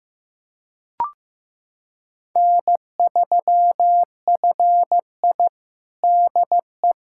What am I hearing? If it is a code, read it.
Morse code, heard as a keyed tone, about 16 wpm, 700 Hz. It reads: N3FI DE
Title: USA Callsign DE Callsign 15wpm